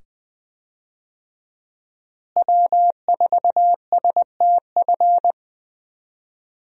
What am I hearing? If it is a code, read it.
Morse code, heard as a keyed tone, about 20 wpm, 700 Hz. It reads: W4STF
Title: USA Call Signs - 5 Characters - Repeated 2x - 20wpm